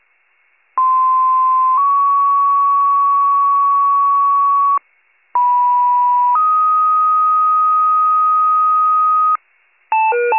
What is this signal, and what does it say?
Two-tone page, heard as a one-second tone, then a three-second tone, about 1040 then 1120 Hz, about 980 then 1250 Hz. It is heard from 0.8 to 9.4 s.